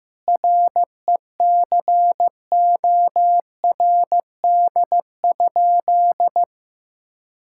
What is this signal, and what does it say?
Morse code, heard as a keyed tone, about 16 wpm, 700 Hz. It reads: RECORD?
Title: Sentences from Top 1000 Words - Review Part 6 - 15wpm